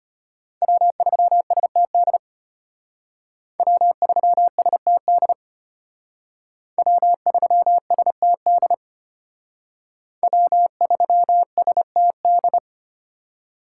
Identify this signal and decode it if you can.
Morse code, heard as a keyed tone, about 32 wpm, 700 Hz. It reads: W3STD W3STD W3STD W3STD